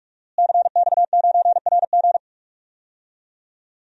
Morse code, heard as a keyed tone, about 46 wpm, 700 Hz. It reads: CX9RG
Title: DXCC-100 Call Signs - 5 Characters - Repeated 2x - 45wpm